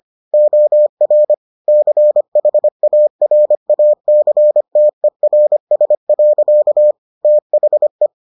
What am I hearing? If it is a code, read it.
Morse code, heard as a keyed tone, about 25 wpm, 600 Hz. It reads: OR CHARACTERS. THE